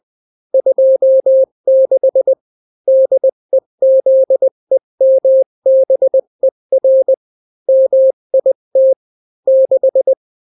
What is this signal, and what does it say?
Morse code, heard as a keyed tone, about 20 wpm, 530 Hz. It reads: 26 DEZEMBER MIT 6
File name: daily / 18wpm